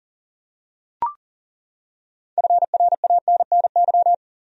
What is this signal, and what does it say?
Morse code, heard as a keyed tone, about 40 wpm, 700 Hz. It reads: FRANNY
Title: USA Common Names - 40wpm